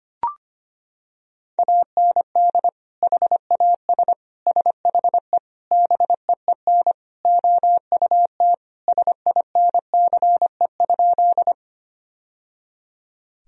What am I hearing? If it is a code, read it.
Morse code, heard as a keyed tone, about 25 wpm, 700 Hz. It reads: AND HAS SHE BEEN OUT SINCE?
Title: Sentences from Top 1000 Words - Review Part 1 - 25wpm